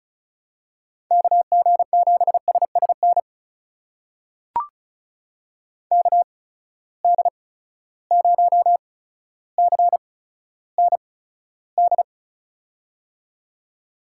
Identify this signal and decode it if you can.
Morse code, heard as a keyed tone, about 36 wpm, spaced as for 15 wpm, 700 Hz. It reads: KG7SSN KD0CND
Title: USA Call Signs - 6 Characters - 8x Character Spacing - 35wpm